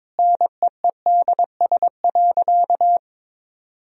Morse code, heard as a keyed tone, about 21 wpm, 700 Hz. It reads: NEEDS.
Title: Sentences from Top 1000 Words - Review Part 3 - 22wpm